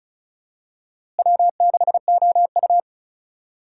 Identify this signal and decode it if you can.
Morse code, heard as a keyed tone, about 36 wpm, 700 Hz. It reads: W6OU
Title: USA Call Signs - 4 Characters - Repeated 3x - 35wpm